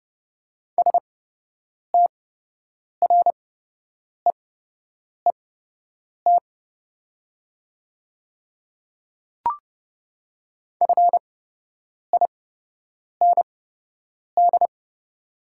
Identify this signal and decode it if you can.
Morse code, heard as a keyed tone, about 29 wpm, spaced as for 8 wpm, 690 Hz. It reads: STREET FIND
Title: 8x Character Spacing - Top 500 Words - 30wpm